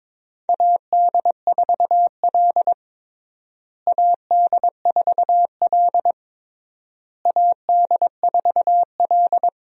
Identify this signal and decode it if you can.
Morse code, heard as a keyed tone, about 22 wpm, 700 Hz. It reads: AD4L AD4L AD4L
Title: USA Call Signs - 4 Characters - Repeated 3x - 22wpm